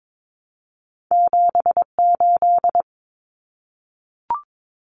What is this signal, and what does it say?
Morse code, heard as a keyed tone, about 22 wpm, 700 Hz. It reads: 78
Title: Common Cut Numbers - 2-Digit 22wpm